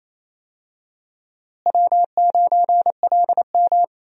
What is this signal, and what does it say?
Morse code, heard as a keyed tone, about 28 wpm, 700 Hz. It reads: W9LM
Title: USA Call Signs - 4 Characters - Repeated 2x - 28wpm